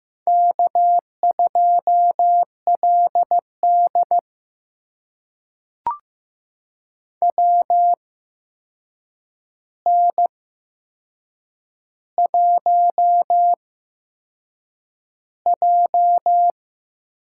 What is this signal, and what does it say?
Morse code, heard as a keyed tone, about 15 wpm, spaced as for 6 wpm, 700 Hz. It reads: K2LD WN1J
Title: USA Call Signs - 8x Character Spacing - 15wpm